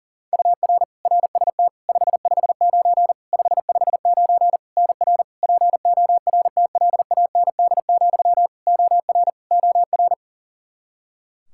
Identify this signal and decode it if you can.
Morse code, heard as a keyed tone, about 40 wpm, 700 Hz. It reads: UR RST 559 559 NR PORTLAND, OR OR
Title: Practice QSO - Exchange 1 - Beginning - 40wpm